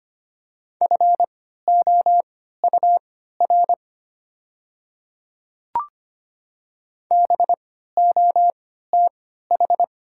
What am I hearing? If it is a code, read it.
Morse code, heard as a keyed tone, about 25 wpm, spaced as for 14 wpm, 690 Hz. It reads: FOUR BOTH
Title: 3x Character Spacing - Top 500 Words - 25wpm